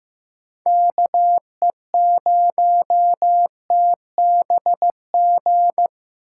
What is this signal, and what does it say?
Morse code, heard as a keyed tone, about 15 wpm, 700 Hz. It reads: KE0TBG
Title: USA Call Signs - 6 Characters - 3x Character Spacing - 15wpm